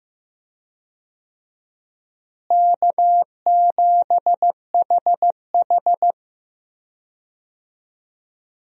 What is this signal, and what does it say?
Morse code, heard as a keyed tone, about 15 wpm, 700 Hz. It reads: K7HH
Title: USA Call Signs - 4 Characters - Repeated 2x - 15wpm